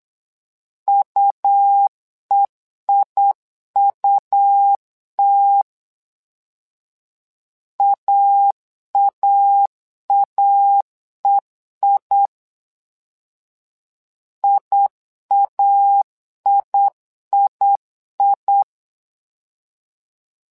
Morse code, heard as a keyed tone, about 8 wpm, 800 Hz. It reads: UEIUT AAAEI IAIII
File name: LEZIONI / 2 UHDN / 2 U 8wpm 2028